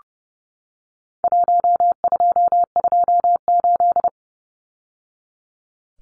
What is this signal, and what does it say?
Morse code, heard as a keyed tone, about 30 wpm, 700 Hz. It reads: 1228
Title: Numbers - Contest Style - 4-Digit 30wpm